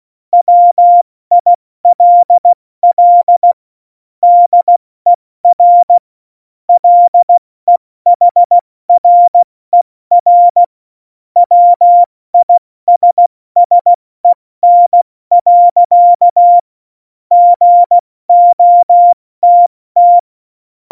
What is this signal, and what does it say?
Morse code, heard as a keyed tone, about 16 wpm, 700 Hz. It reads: WILL DER LEHRER WISSEN. GOTT
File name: Morsekurs / Witze / Witz 26. Februar 2019, 70 F80